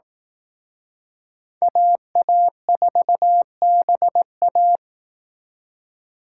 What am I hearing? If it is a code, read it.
Morse code, heard as a keyed tone, about 18 wpm, 700 Hz. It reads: AA4BA
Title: Speed Racing - USA Call Signs 15wpm